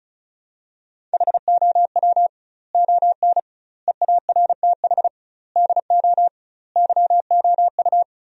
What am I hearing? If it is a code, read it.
Morse code, heard as a keyed tone, about 34 wpm, 700 Hz. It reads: HOW ON EARTH DO YOU
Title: 2x Word-Spacing - Sentences from Top 200 Words - 35wpm